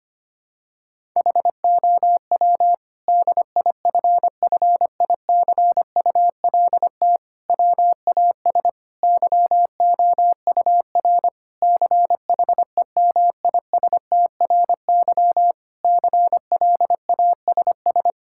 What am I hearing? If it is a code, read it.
Morse code, heard as a keyed tone, about 25 wpm, 700 Hz. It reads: HOW DIFFICULT WAS YOUR CHEMISTRY CLASS
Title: Sentences from Top 1000 Words - Review Part 6 - 25wpm